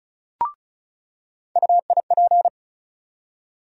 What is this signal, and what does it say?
Morse code, heard as a keyed tone, about 34 wpm, 700 Hz. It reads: UIP
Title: USA Call Sign Postfixes - 3 Letters 35wpm